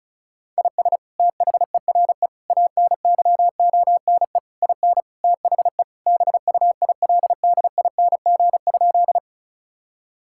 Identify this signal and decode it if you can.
Morse code, heard as a keyed tone, about 34 wpm, 700 Hz. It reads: IS THERE ANYONE IN THE BUILDING?
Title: Sentences from Top 1000 Words - Review Part 5 - 35wpm